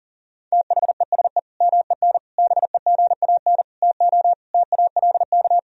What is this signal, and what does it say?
Morse code, heard as a keyed tone, about 40 wpm, 700 Hz. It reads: THESE MEN BEGAN TO TALK
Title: Sentences from Top 1000 Words - Review Part 1 - 40wpm